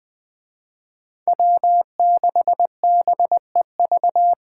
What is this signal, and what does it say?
Morse code, heard as a keyed tone, about 21 wpm, 700 Hz. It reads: W6BEV
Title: Speed Racing - USA 1x3 Call Signs 17wpm